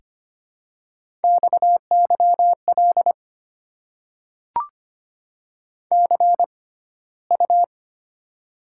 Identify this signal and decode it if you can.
Morse code, heard as a keyed tone, about 25 wpm, spaced as for 15 wpm, 700 Hz. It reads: XYL CU
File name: Top 90 QSO Elements - 6x Character Spacing - 25wpm